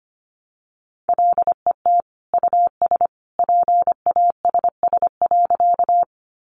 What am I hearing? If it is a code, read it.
Morse code, heard as a keyed tone, about 25 wpm, 700 Hz. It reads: LET US PASS.
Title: Sentences from Top 400 Words 25wpm